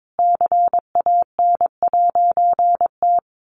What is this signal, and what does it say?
Morse code, heard as a keyed tone, about 22 wpm, 700 Hz. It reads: CAN'T
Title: A Sound Of Thunder 22wpm